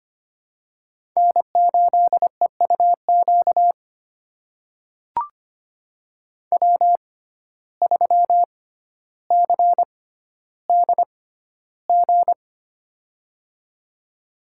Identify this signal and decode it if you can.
Morse code, heard as a keyed tone, about 25 wpm, spaced as for 13 wpm, 700 Hz. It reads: N8EUQ W3CDG
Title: USA Call Signs - 5 Characters - 6x Character Spacing - 25wpm